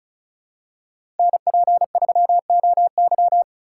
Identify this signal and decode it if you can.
Morse code, heard as a keyed tone, about 36 wpm, 700 Hz. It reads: NP3OY
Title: USA Call Signs - 2x Character Spacing - 35wpm